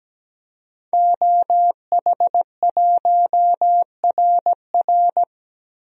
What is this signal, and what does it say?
Morse code, heard as a keyed tone, about 17 wpm, 700 Hz. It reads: OH1RR
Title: DXCC-100 Call Signs - 5 Characters - Repeated 3x - 17wpm